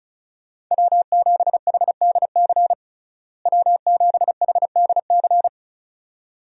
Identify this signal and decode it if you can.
Morse code, heard as a keyed tone, about 34 wpm, 700 Hz. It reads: W7HDC W7HDC
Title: USA Call Signs - 5 Characters - Repeated 2x - 35wpm